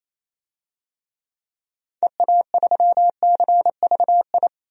Morse code, heard as a keyed tone, about 27 wpm, 700 Hz. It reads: EA3CVI
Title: DXCC-100 Call Signs - 6 Characters - 28wpm